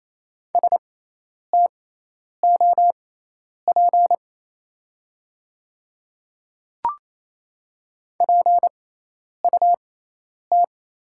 Morse code, heard as a keyed tone, about 28 wpm, spaced as for 10 wpm, 700 Hz. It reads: STOP PUT
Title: 6x Character Spacing - Top 500 Words - 28wpm